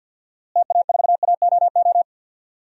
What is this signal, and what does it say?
Morse code, heard as a keyed tone, about 52 wpm, 700 Hz. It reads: TA4AOO
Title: DXCC-100 Call Signs - 6 Characters - Repeated 3x - 50wpm